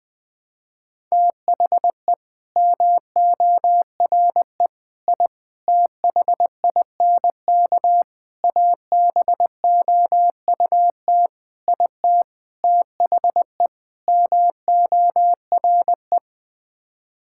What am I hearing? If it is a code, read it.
Morse code, heard as a keyed tone, about 20 wpm, 700 Hz. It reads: THE MORE I THINK ABOUT IT THE MORE
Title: Sets of 8 Words - 20wpm